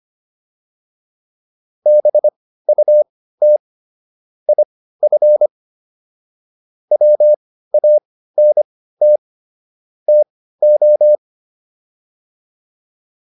Morse code, heard as a keyed tone, about 25 wpm, spaced as for 15 wpm, 600 Hz. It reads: BUT IF WANT TO
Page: intermediate-practice-files